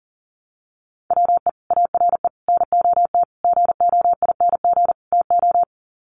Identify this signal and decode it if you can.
Morse code, heard as a keyed tone, about 37 wpm, 700 Hz. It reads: WE ARE NOT GOING TO
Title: Sets of 5 Words 40wpm